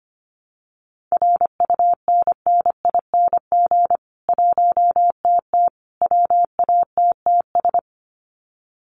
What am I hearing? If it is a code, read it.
Morse code, heard as a keyed tone, about 25 wpm, 700 Hz. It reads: RUNNING 1TT WATTS
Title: Common QSO Phrases 25wpm